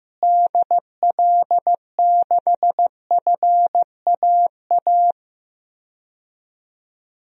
Morse code, heard as a keyed tone, about 15 wpm, 700 Hz. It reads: DL6FAA
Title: DXCC-100 Call Signs - 6 Characters - Repeated 3x - 15wpm